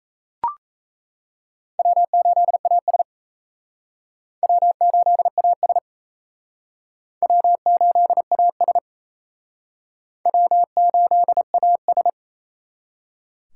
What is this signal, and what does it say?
Morse code, heard as a keyed tone, about 34 wpm, 700 Hz. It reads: W8AS W8AS W8AS W8AS
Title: Speed Racing - USA 1x2 Call Signs 28wpm